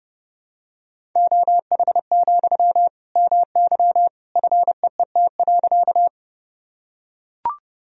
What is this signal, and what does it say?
Morse code, heard as a keyed tone, about 32 wpm, 700 Hz. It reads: OH, MY FEET.
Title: Sentences from Top 1000 Words - Review Part 1 - 30wpm